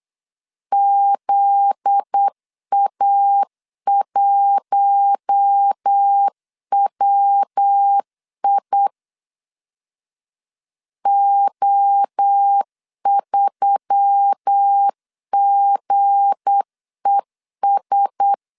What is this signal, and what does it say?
Morse code, heard as a keyed tone, about 8 wpm, 800 Hz. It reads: ZA1WI O3GES
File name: LEZIONI / 8 YJ45 / 8 Y 8wpm 3245